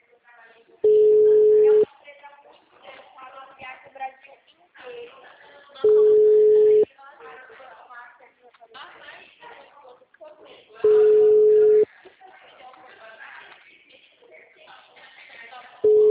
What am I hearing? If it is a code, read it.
Ringback tone (Europe, 425 Hz), with speech.
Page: Itatiba - SP